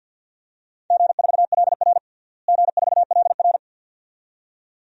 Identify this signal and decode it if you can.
Morse code, heard as a keyed tone, about 50 wpm, 700 Hz. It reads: G4LR G4LR
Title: DXCC-100 Call Signs - 4 Characters - Repeated 2x - 50wpm